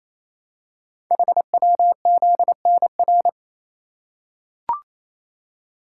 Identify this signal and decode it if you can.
Morse code, heard as a keyed tone, about 27 wpm, 700 Hz. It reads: HWZNR
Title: Code Groups - 5 Letter:Numbers - 28wpm